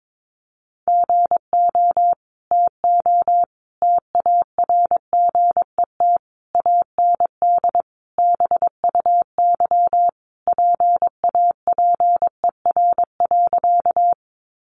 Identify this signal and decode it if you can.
Morse code, heard as a keyed tone, about 22 wpm, 700 Hz. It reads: GO TO TARGET AND BUY PAPER.
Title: Sentences from Top 1200 Words 22wpm